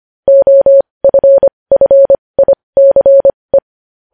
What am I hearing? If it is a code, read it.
Morse code, heard as a keyed tone, about 25 wpm, spaced as for 20 wpm, 550 Hz. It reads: OFFICE
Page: News Headlines in Morse Code at 20 WPM